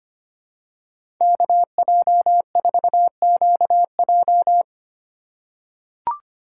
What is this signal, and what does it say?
Morse code, heard as a keyed tone, about 25 wpm, 700 Hz. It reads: KJ4QJ
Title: USA Call Signs - 5 Characters - 4x Character Spacing - 25wpm